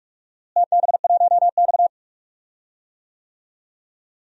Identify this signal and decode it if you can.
Morse code, heard as a keyed tone, about 46 wpm, 700 Hz. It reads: TB1X
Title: DXCC-100 Call Signs - 4 Characters - Repeated 2x - 45wpm